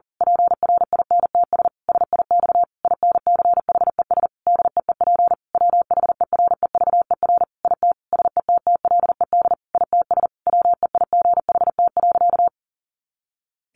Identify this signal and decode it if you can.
Morse code, heard as a keyed tone, about 40 wpm, 700 Hz. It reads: PRINTS SIX INCHES DEEP WHEREVER IT SETTLED ITS WEIGHT.
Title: A Sound Of Thunder 40wpm